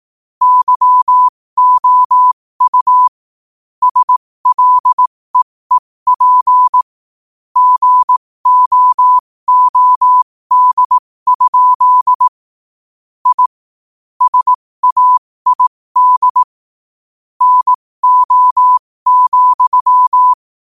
Morse code, heard as a keyed tone, about 18 wpm, spaced as for 15 wpm, 1000 Hz. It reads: YOU SLEEP GOOD? I SAID NO,
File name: qotdfiles / 15wpm